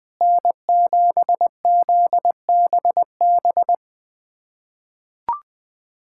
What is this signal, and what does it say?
Morse code, heard as a keyed tone, about 20 wpm, 700 Hz. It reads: N7ZBB
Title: USA Callsign DE Callsign - 5 Characters - 20wpm